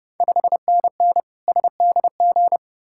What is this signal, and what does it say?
Morse code, heard as a keyed tone, about 30 wpm, 700 Hz. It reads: 5NN SDG
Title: Ontario QSO Party Exchange 30wpm